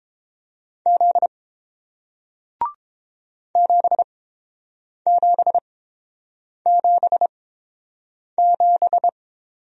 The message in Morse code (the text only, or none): Z 7777